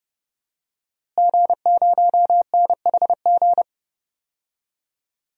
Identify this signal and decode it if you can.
Morse code, heard as a keyed tone, about 30 wpm, 700 Hz. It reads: G0NHG
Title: DXCC-100 Call Signs - 5 Characters - Repeated 2x - 30wpm